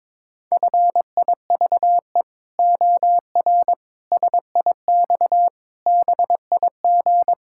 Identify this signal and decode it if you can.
Morse code, heard as a keyed tone, about 22 wpm, 700 Hz. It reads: FIVE OR SIX BIG
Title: Pangram Challenge v2 - 22wpm